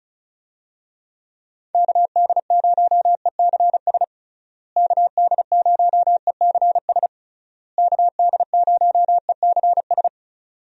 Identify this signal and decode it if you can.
Morse code, heard as a keyed tone, about 34 wpm, 700 Hz. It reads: KD0ECS KD0ECS KD0ECS
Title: USA Call Signs - 6 Characters - Repeated 3x - 35wpm